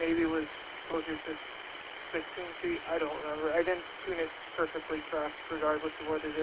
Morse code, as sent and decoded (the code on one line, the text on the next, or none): none